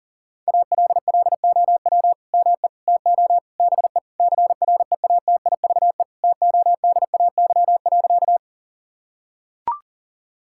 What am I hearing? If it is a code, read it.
Morse code, heard as a keyed tone, about 40 wpm, 700 Hz. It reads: ALLOW ME TO BE CREATIVE TODAY.
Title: Sentences from Top 1000 Words - Review Part 5 - 40wpm